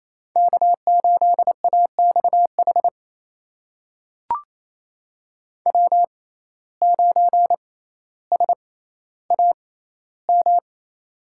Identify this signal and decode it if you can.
Morse code, heard as a keyed tone, about 27 wpm, spaced as for 14 wpm, 700 Hz. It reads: K8AXH W9SAM
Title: USA Call Signs - 5 Characters - 6x Character Spacing - 28wpm